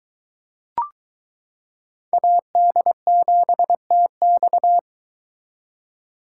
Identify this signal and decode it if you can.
Morse code, heard as a keyed tone, about 22 wpm, 700 Hz. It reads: AD7TX